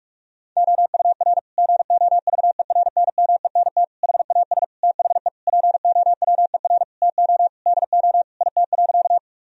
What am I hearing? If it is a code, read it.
Morse code, heard as a keyed tone, about 41 wpm, 700 Hz. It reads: OUR GOVERNMENT HAS THE POWER TO DO IT.
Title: Sentences from Top 1000 Words - Review Part 3 - 45wpm